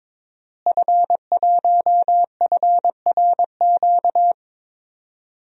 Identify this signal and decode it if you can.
Morse code, heard as a keyed tone, about 22 wpm, 700 Hz. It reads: F1FRQ
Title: DXCC-100 Call Signs - 5 Characters - Repeated 3x - 22wpm